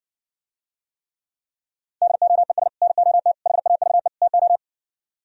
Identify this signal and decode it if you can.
Morse code, heard as a keyed tone, about 62 wpm, 700 Hz. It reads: DOES NOT HAVE TO